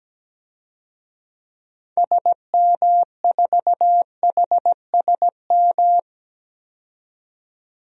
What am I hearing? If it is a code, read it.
Morse code, heard as a keyed tone, about 17 wpm, 700 Hz. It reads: SM4HSM
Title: DXCC-100 Call Signs - 6 Characters - 17wpm